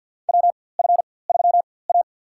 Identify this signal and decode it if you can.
Morse code, heard as a keyed tone, about 52 wpm, spaced as for 23 wpm, 700 Hz. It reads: WF3A
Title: USA Call Signs - 4x Character Spacing - 50wpm